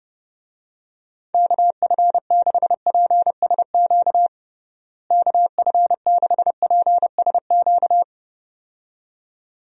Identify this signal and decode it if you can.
Morse code, heard as a keyed tone, about 30 wpm, 700 Hz. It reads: KF6PSQ KF6PSQ
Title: USA Call Signs - 6 Characters - Repeated 2x - 30wpm